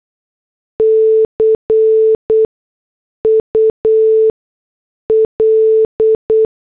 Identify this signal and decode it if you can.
Morse code, heard as a keyed tone, about 8 wpm, spaced as for 6 wpm, 440 Hz. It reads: CUL